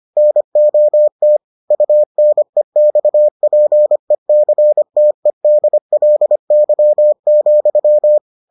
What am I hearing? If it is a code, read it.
Morse code, heard as a keyed tone, about 25 wpm, 600 Hz. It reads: NOT UNEXPECTEDLY,